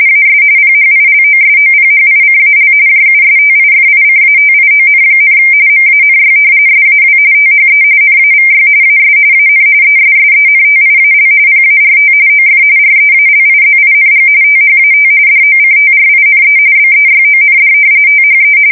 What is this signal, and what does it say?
RTTY, about 45.45 baud. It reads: AS PHILOSOPHICAL JUSTIFICATION, MANY SAW THE NAVY AS AN ESSENTIAL PART OF REALIZING THAT DOCTRINE BEYOND THE
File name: fsk170 / usnavy3